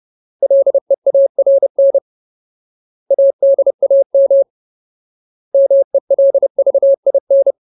Morse code, heard as a keyed tone, about 30 wpm, 550 Hz. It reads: LEARN ADAM MELVIN
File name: Ultimate Word List / 30WPM Ultimate